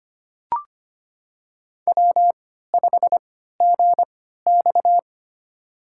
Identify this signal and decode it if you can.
Morse code, heard as a keyed tone, about 25 wpm, spaced as for 14 wpm, 700 Hz. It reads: W5GX
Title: USA Call Signs - 4 Characters - 3x Character Spacing - 25wpm